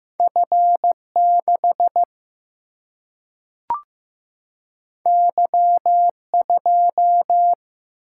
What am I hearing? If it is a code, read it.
Morse code, heard as a keyed tone, about 15 wpm, 700 Hz. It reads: F6 Y2
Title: Single-Letter Single-Number 15wpm